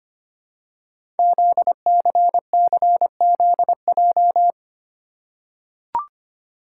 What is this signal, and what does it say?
Morse code, heard as a keyed tone, about 25 wpm, 700 Hz. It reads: ZCCZJ
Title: Code Groups - 5 Letters - 25wpm